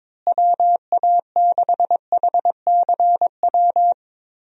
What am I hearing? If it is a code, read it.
Morse code, heard as a keyed tone, about 22 wpm, 700 Hz. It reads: WA6HCW
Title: USA Call Signs - 6 Characters - Repeated 2x - 22wpm